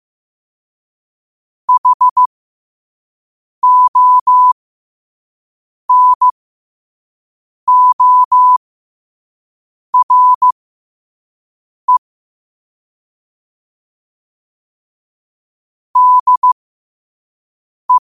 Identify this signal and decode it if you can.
Morse code, heard as a keyed tone, about 15 wpm, spaced as for 5 wpm, 1000 Hz. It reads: HONORE DE
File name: qotdfiles / 5wpm